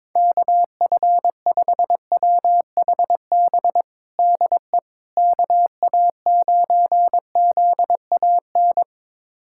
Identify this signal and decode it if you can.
Morse code, heard as a keyed tone, about 22 wpm, 700 Hz. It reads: KF5WHB DE KA9ZAN
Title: USA Callsign DE Callsign - 6 Characters - 22wpm